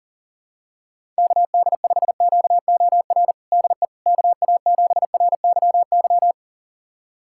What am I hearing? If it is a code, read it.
Morse code, heard as a keyed tone, about 40 wpm, 700 Hz. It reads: KD5QOR DE KA7RYY
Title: USA Callsign DE Callsign - 6 Characters - 40wpm